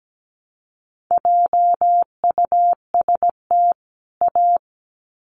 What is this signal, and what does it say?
Morse code, heard as a keyed tone, about 17 wpm, 700 Hz. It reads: JUST A
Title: Sets of 2 Words 17wpm